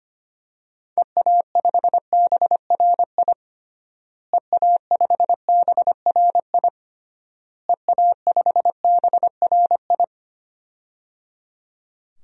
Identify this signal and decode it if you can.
Morse code, heard as a keyed tone, about 25 wpm, 700 Hz. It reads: EA5BRI EA5BRI EA5BRI
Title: DXCC-100 Call Signs - 6 Characters - Repeated 3x - 25wpm